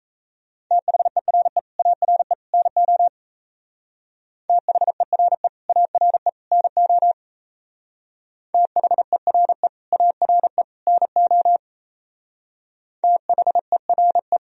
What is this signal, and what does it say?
Morse code, heard as a keyed tone, about 35 wpm, 700 Hz. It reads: THERE ARE NO THERE ARE NO THERE ARE NO THERE
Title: Speed-Racing - Sets of 3 Words 28wpm